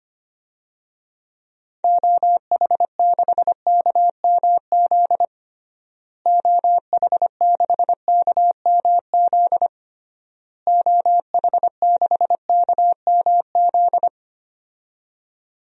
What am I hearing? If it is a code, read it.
Morse code, heard as a keyed tone, about 25 wpm, 700 Hz. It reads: OH6KMZ OH6KMZ OH6KMZ
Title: DXCC-100 Call Signs - 6 Characters - Repeated 3x - 25wpm